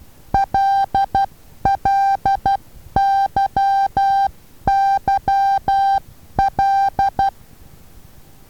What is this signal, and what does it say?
Morse code, heard as a keyed tone, about 12 wpm, 790 Hz. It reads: LLYYL
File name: Taviro HA6VA B